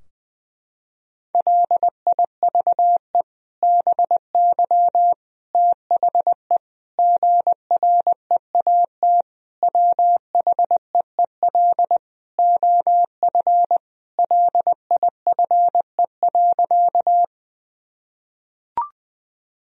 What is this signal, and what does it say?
Morse code, heard as a keyed tone, about 21 wpm, 700 Hz. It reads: LIVE BY THE GREAT WHEEL OF LIFE.
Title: Sentences from Top 1000 Words - Review Part 3 - 20wpm